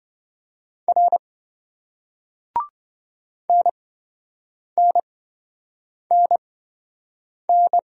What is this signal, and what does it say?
Morse code, heard as a keyed tone, about 27 wpm, spaced as for 7 wpm, 700 Hz. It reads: R NNNN